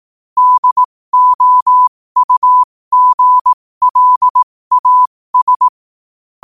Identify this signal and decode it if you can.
Morse code, heard as a keyed tone, about 18 wpm, spaced as for 15 wpm, 1000 Hz. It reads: DOUGLAS
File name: qotdfiles / 15wpm